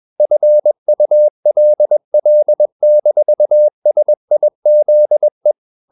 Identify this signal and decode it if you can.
Morse code, heard as a keyed tone, about 21 wpm, 600 Hz. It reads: FULL-SIZE
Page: CW med Gnister